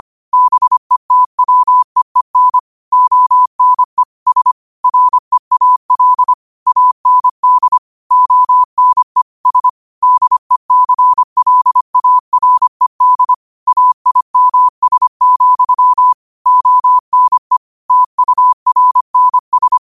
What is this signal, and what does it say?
Morse code, heard as a keyed tone, about 25 wpm, 1000 Hz. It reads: BETWEEN ONE S REAL AND ONE S DECLARED AIMS, ONE TURNS